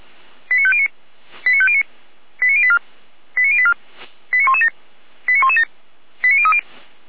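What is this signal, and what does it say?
Five-tone selective call (ZVEI), heard as quick runs of five tones, tones of about 70 ms. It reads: 89509 89509 89084 89084 89117 89117 89030
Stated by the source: ZVEI five-tone selective call received on SDR.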